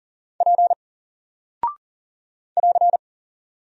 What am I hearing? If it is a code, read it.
Morse code, heard as a keyed tone, about 40 wpm, 700 Hz. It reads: P <AR>
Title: Single Character (Letter:Number:Punctuation:Prosign) 40wpm